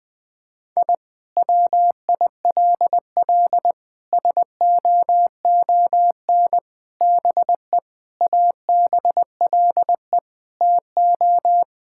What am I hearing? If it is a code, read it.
Morse code, heard as a keyed tone, about 21 wpm, 700 Hz. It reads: I WILL SOON BE ABLE TO
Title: Sentences from Top 1000 Words - Review Part 6 - 20wpm